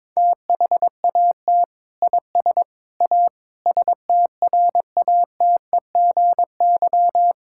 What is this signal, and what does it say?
Morse code, heard as a keyed tone, about 22 wpm, 700 Hz. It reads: THAT IS A STRATEGY
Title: Sentences from Top 1000 Words - Review Part 5 - 22wpm